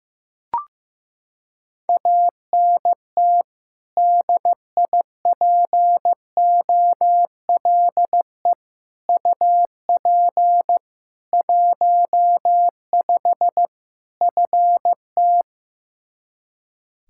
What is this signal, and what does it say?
Morse code, heard as a keyed tone, about 16 wpm, 700 Hz. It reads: ANT DIPOLE UP 15 FT
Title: Practice QSO - Exchange 2 - Info - 15wpm